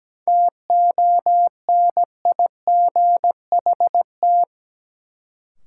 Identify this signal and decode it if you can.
Morse code, heard as a keyed tone, about 17 wpm, 700 Hz. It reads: TONIGHT
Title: Sentences from Top 600 Words - New Word Drill - 17wpm